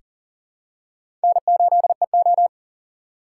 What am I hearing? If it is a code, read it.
Morse code, heard as a keyed tone, about 40 wpm, 700 Hz. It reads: N8EO
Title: USA Call Signs - 4 Characters - 3x Character Spacing - 40wpm